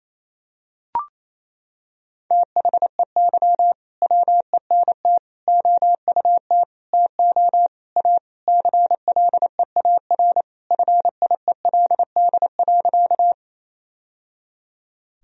Morse code, heard as a keyed tone, about 28 wpm, 700 Hz. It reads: THEY WENT OUT TO A CLEAR FIELD.
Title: Sentences from Top 1000 Words - Review Part 2 - 28wpm